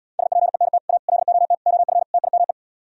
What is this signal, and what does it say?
Morse code, heard as a keyed tone, about 77 wpm, 700 Hz. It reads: SPENT A LOT OF TIME